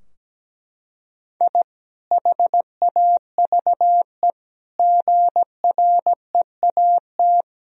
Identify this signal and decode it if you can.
Morse code, heard as a keyed tone, about 17 wpm, 700 Hz. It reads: I HAVE GREAT
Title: Sentences from Top 1000 Words - Review Part 2 - 17wpm